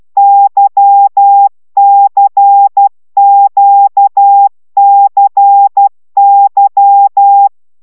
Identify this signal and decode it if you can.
Morse code, heard as a keyed tone, about 12 wpm, 800 Hz. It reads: YCQCY
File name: taviro tanulas / cw 75 15 10